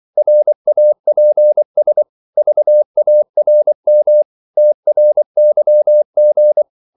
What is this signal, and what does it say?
Morse code, heard as a keyed tone, about 24 wpm, 600 Hz. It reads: RAPS VARM TRYG
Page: Korte danske ord 24wpm